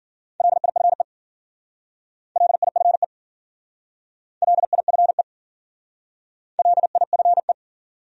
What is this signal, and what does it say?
Morse code, heard as a keyed tone, about 50 wpm, 700 Hz. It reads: LIFE LIFE LIFE LIFE
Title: Speed-Racing Top 500 Words 40wpm